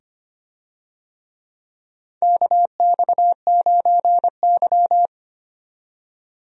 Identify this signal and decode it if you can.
Morse code, heard as a keyed tone, about 25 wpm, 700 Hz. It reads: KX9Y
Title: USA Call Signs - 4 Characters - 25wpm